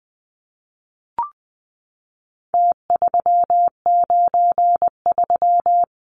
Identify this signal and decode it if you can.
Morse code, heard as a keyed tone, about 20 wpm, 700 Hz. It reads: T393
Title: Numbers - Contest Style - 4-Digit 20wpm